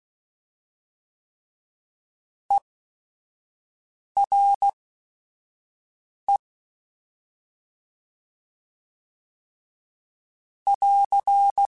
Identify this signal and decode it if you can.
Morse code, heard as a keyed tone, about 16 wpm, spaced as for 5 wpm, 2950 Hz. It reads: ERE <AR>
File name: Short Runs / Short Runs-mp3s / ESE ERE @5